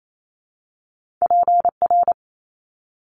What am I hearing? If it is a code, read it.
Morse code, heard as a keyed tone, about 28 wpm, 700 Hz. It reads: PR
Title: Top Two Letter Combinations in Words 28wpm